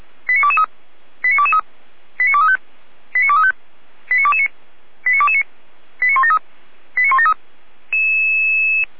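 Five-tone selective call (ZVEI), heard as quick runs of five tones, tones of about 70 ms. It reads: 89223 89223 89235 89235 89229 89229 89173 89173
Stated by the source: ZVEI five-tone selective call 89208 received on an SDR.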